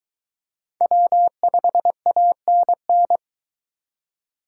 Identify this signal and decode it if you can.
Morse code, heard as a keyed tone, about 21 wpm, 700 Hz. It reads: W5ANN